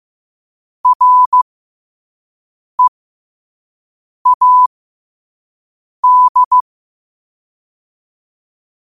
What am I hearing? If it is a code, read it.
Morse code, heard as a keyed tone, about 15 wpm, spaced as for 5 wpm, 1000 Hz. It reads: READ
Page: Quotes for Wed, 13 Aug 2025 in Morse Code at 5 words per minute.